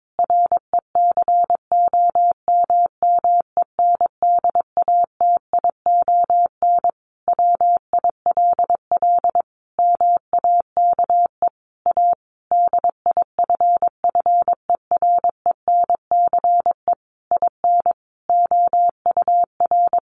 Morse code, heard as a keyed tone, about 22 wpm, 700 Hz. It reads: RECOMMENDATION WILL MAKE A DIFFERENCE IN OUR
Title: Sentences from Top 1400 Words 22wpm